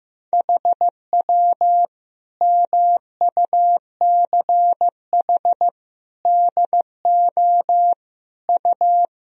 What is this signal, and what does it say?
Morse code, heard as a keyed tone, about 15 wpm, 700 Hz. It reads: HW MUCH DO U
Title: Sentences from Top 200 Words and Abbreviations 15wpm